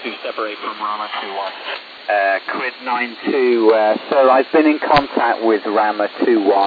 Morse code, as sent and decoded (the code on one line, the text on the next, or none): none